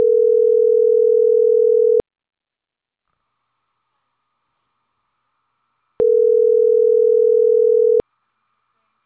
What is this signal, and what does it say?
Ringback tone (North America), 440 + 480 Hz.